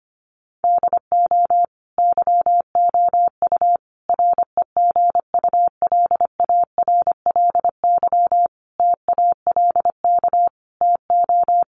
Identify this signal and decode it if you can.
Morse code, heard as a keyed tone, about 25 wpm, 700 Hz. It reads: DO YOU REGULARLY TALK TO
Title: Sentences from Top 1400 Words 25wpm